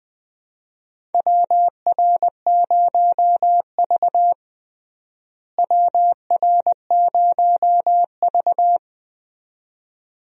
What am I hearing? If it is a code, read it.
Morse code, heard as a keyed tone, about 20 wpm, 700 Hz. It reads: WR0V WR0V
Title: USA Call Signs - 4 Characters - Repeated 2x - 20wpm